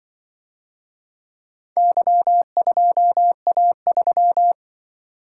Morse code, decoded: Y2A3

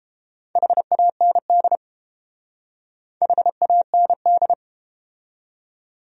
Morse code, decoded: HAND HAND

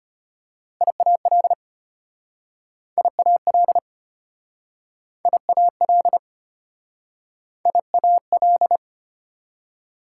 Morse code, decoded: IAL IAL IAL IAL

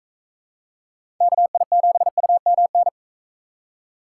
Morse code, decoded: KI7UMN

